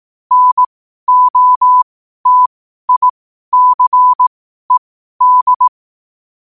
Morse code, heard as a keyed tone, about 18 wpm, spaced as for 12 wpm, 1000 Hz. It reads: NOTICED
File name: 12wpm